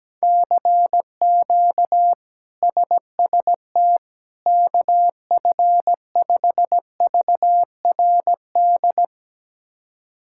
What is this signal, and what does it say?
Morse code, heard as a keyed tone, about 17 wpm, 700 Hz. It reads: CQ SST KF5VRD
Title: K1USN SST QSO Practice - 17wpm